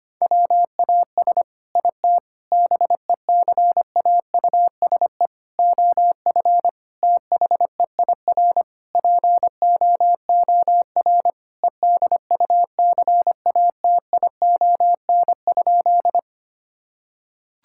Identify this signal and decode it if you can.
Morse code, heard as a keyed tone, about 25 wpm, 700 Hz. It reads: WAS IT BECAUSE OF THEIR POOR EDUCATION?
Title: Sentences from Top 1000 Words - Review Part 6 - 25wpm